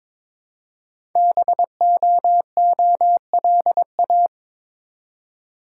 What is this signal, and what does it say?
Morse code, heard as a keyed tone, about 23 wpm, 700 Hz. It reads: BOOLA
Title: Code Groups - 5 Letters - 22wpm